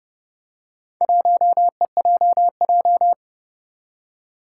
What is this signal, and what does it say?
Morse code, heard as a keyed tone, about 28 wpm, 700 Hz. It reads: 1EJJ
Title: Four Challenging Letter-Number Combinations 30wpm